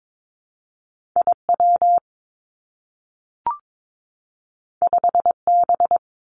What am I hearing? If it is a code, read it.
Morse code, heard as a keyed tone, about 22 wpm, 700 Hz. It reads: I W 5 B
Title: Two Challenging Letter-Number Combinations 22wpm